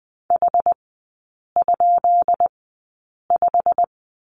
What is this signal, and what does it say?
Morse code, heard as a keyed tone, about 20 wpm, spaced as for 8 wpm, 690 Hz. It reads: H?5
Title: 133 Lesson 22e - Question Mark - Three Characters 20wpm